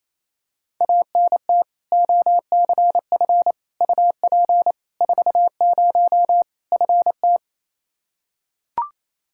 Morse code, heard as a keyed tone, about 28 wpm, 700 Hz. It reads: ANT OCF UP 40 FT
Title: K1USN SST QSO Practice 28wpm